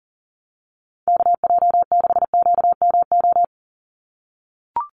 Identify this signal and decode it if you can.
Morse code, heard as a keyed tone, about 40 wpm, 700 Hz. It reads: KJ6QMO
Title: USA Call Signs 40wpm